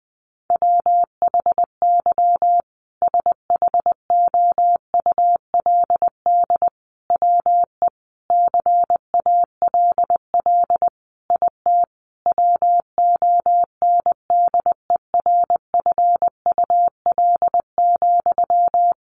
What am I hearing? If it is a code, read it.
Morse code, heard as a keyed tone, about 20 wpm, 700 Hz. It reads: WHY SHOULD WE CALL IT WONDERFUL,
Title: Sentences from Top 1100 Words 20wpm